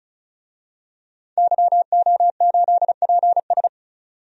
Morse code, heard as a keyed tone, about 36 wpm, 700 Hz. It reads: YO8PS
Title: DXCC-100 Call Signs - 5 Characters - 35wpm